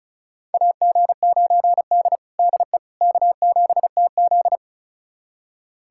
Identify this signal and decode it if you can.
Morse code, heard as a keyed tone, about 36 wpm, 700 Hz. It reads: AG9D DE K7TZ
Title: USA Callsign DE Callsign - 4 Characters - 35wpm